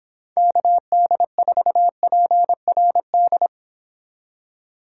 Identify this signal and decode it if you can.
Morse code, heard as a keyed tone, about 26 wpm, 700 Hz. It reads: KD4PRD